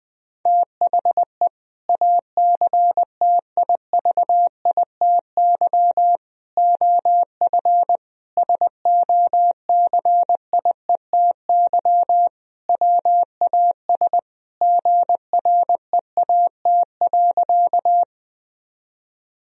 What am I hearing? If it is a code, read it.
Morse code, heard as a keyed tone, about 20 wpm, 700 Hz. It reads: THE ACTIVITY OF SOCIETY WAS GREAT.
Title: Sentences from Top 1000 Words - Review Part 4 - 20wpm